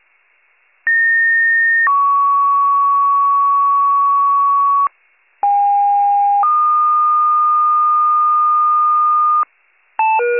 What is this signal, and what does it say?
Two-tone page, heard as a one-second tone, then a three-second tone, about 1850 then 1120 Hz, about 810 then 1220 Hz. It is heard from 0.9 to 9.4 s.